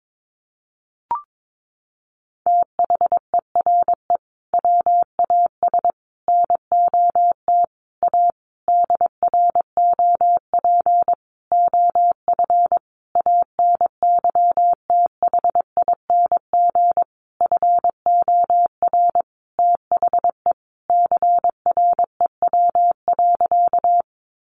Morse code, heard as a keyed tone, about 22 wpm, 700 Hz. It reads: THERE WAS NOT A DROP OF ANYTHING FOR THE CREW.
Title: Sentences from Top 1500 Words 22wpm